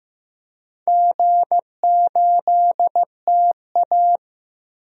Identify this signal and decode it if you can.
Morse code, heard as a keyed tone, about 15 wpm, 700 Hz. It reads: G8TA
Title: DXCC-100 Call Signs - 4 Characters - Repeated 3x - 15wpm